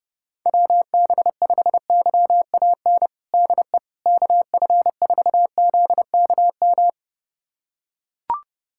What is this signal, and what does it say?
Morse code, heard as a keyed tone, about 30 wpm, 700 Hz. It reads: WB5YAN DE KF4ZKM
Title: USA Callsign DE Callsign 30wpm